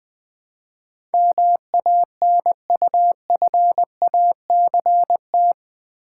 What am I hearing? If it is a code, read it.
Morse code, heard as a keyed tone, about 20 wpm, 700 Hz. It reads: MANUFACT
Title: Build A Word - 4 Letters At A Time - 11-Letter Words - 20wpm